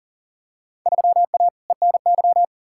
Morse code, heard as a keyed tone, about 40 wpm, 700 Hz. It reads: 3A ENY
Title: Field Day Exchanges - 40wpm